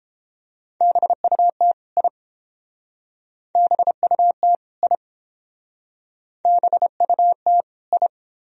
Morse code, heard as a keyed tone, about 31 wpm, 700 Hz. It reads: BUT I BUT I BUT I